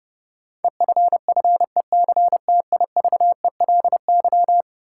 Morse code, heard as a keyed tone, about 30 wpm, 700 Hz. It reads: EFFECTIVELY
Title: Sentences from Top 1400 Words - New Word Drill - 30wpm